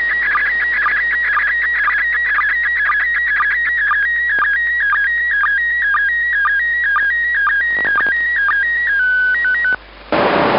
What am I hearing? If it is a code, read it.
SSTV (slow-scan television), PD 120.
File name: ik1yrc / AUDIO FOR QRZ / ISS 27 12 2020 0805 orbit2194